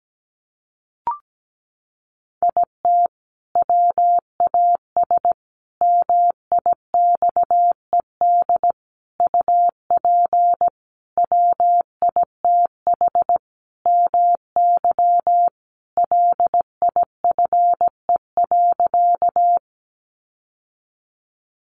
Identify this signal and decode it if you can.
Morse code, heard as a keyed tone, about 17 wpm, 700 Hz. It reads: IT WAS MIXED UP WITH MY LIFE.
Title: Sentences from Top 1500 Words 17wpm